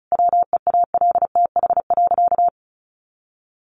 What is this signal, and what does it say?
Morse code, heard as a keyed tone, about 34 wpm, 700 Hz. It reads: WEALTH.
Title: Sentences from Top 1200 Words 35wpm